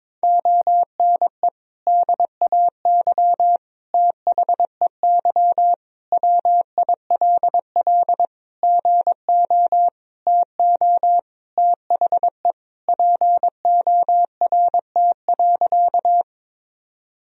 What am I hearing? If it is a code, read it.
Morse code, heard as a keyed tone, about 22 wpm, 700 Hz. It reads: ONE DAY THEY WILL GO TO THE PORT.
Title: Sentences from Top 1000 Words - Review Part 1 - 22wpm